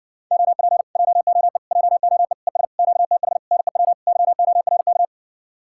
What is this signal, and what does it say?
Morse code, heard as a keyed tone, about 60 wpm, 700 Hz. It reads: OP JOE JOE ES QTH NEW YORK